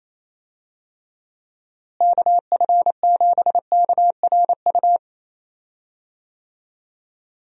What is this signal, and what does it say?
Morse code, heard as a keyed tone, about 28 wpm, 700 Hz. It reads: KF7KRU
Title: USA Call Signs - 6 Characters - Repeated 3x - 28wpm